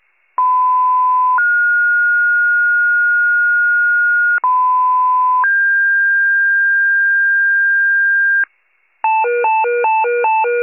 Two-tone page, heard as a one-second tone, then a three-second tone, about 1010 then 1430 Hz, about 1010 then 1600 Hz, with noise. It is heard from 0.4 to 8.4 s.
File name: pager / 94 FIRE / wcalert